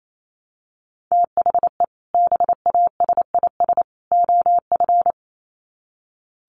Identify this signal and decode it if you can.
Morse code, heard as a keyed tone, about 28 wpm, 700 Hz. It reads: THE BASIS OF
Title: Sets of 3 Words 28wpm